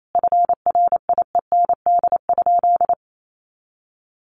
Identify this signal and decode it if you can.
Morse code, heard as a keyed tone, about 28 wpm, 700 Hz. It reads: FRIEND?
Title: Sentences from Top 1000 Words 28wpm